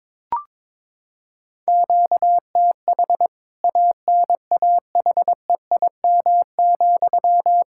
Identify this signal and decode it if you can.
Morse code, heard as a keyed tone, about 22 wpm, 700 Hz. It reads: QTH ANAHEIM,
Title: K1USN SST QSO Practice 22wpm